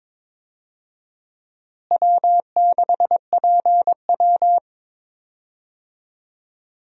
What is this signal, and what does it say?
Morse code, heard as a keyed tone, about 21 wpm, 700 Hz. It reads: W6PW
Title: USA Call Signs - 4 Characters - Repeated 2x - 22wpm